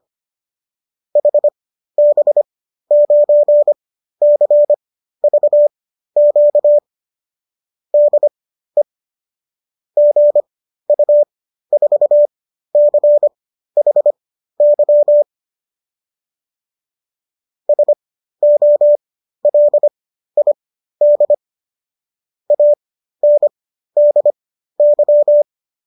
Morse code, heard as a keyed tone, about 25 wpm, spaced as for 13 wpm, 590 Hz. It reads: HB9CVQ DE GU4CHY SOLID ANDY